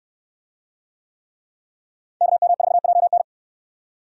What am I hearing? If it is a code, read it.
Morse code, heard as a keyed tone, about 65 wpm, 700 Hz. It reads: KM4JN